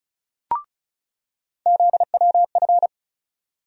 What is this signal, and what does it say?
Morse code, heard as a keyed tone, about 36 wpm, 700 Hz. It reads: ZWF
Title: USA Call Sign Postfixes - 3 Letters 35wpm